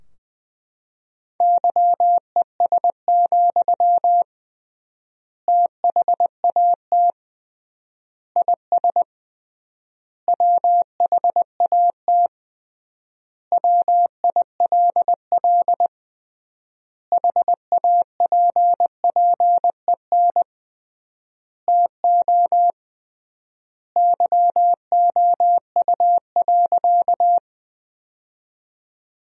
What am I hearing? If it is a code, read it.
Morse code, heard as a keyed tone, about 20 wpm, 700 Hz. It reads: YES, THAT IS WHAT WILL HAPPEN TO YOU.
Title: 3x Word-Spacing - Sentences from Top 500 Words - 20wpm